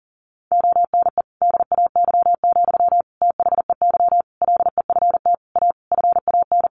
Morse code, heard as a keyed tone, about 40 wpm, 700 Hz. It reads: ONE DAY, THEY LEFT A FAN
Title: Sentences from Top 1300 Words 40wpm